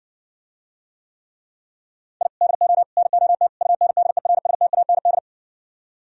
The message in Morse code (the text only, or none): I DO NOT UNDERSTAND